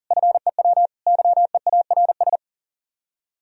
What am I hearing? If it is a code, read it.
Morse code, heard as a keyed tone, about 50 wpm, 700 Hz. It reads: FEW YEARS